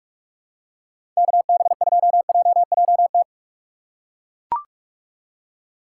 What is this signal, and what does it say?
Morse code, heard as a keyed tone, about 42 wpm, 700 Hz. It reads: KB2JJT
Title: USA Call Signs - 6 Characters - 4x Character Spacing - 45wpm